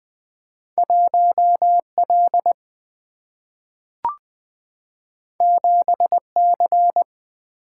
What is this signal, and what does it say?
Morse code, heard as a keyed tone, about 20 wpm, 700 Hz. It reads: 1L 7C
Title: Single-Letter Single-Number 20wpm